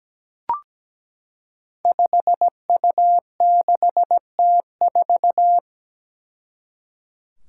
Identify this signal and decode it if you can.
Morse code, heard as a keyed tone, about 17 wpm, 700 Hz. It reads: 5U6T4
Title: Five Challenging Letter-Number Combinations 17wpm